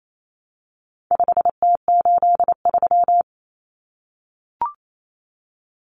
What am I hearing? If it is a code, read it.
Morse code, heard as a keyed tone, about 28 wpm, 700 Hz. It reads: 5T83
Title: Common Cut Numbers - 4-Digit 28wpm